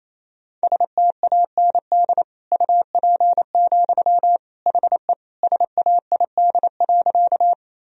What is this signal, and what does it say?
Morse code, heard as a keyed tone, about 28 wpm, 700 Hz. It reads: STAND UP, HE SAID.
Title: Sentences from Top 1000 Words - Review Part 1 - 28wpm